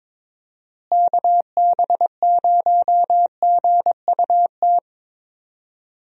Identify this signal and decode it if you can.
Morse code, heard as a keyed tone, about 23 wpm, 700 Hz. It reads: KB0GUT
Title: USA Call Signs - 6 Characters - Repeated 3x - 22wpm